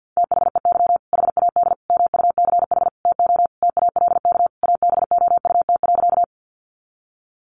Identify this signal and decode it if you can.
Morse code, heard as a keyed tone, about 49 wpm, 700 Hz. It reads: THEY HAD MUCH TO TALK ABOUT.
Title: Sentences from Top 400 Words 50wpm